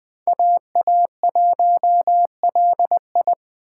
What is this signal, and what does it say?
Morse code, heard as a keyed tone, about 24 wpm, 700 Hz. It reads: AA1LI
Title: Speed Racing - USA 2x2 Call Signs 20wpm